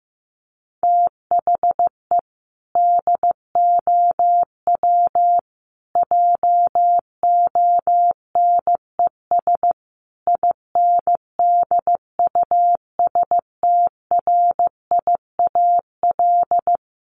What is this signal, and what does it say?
Morse code, heard as a keyed tone, about 15 wpm, 700 Hz. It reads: THE DOW JONES INDUSTRIAL
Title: Sets of 5 Words 15wpm